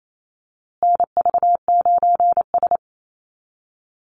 Morse code, heard as a keyed tone, about 28 wpm, 700 Hz. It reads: NV9S
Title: USA Call Signs Repeated 3x 28wpm